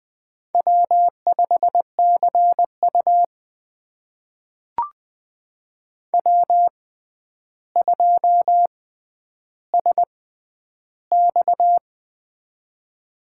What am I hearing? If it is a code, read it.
Morse code, heard as a keyed tone, about 20 wpm, spaced as for 10 wpm, 700 Hz. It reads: W5CU W2SX
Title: USA Call Signs - 4 Characters - 6x Character Spacing - 20wpm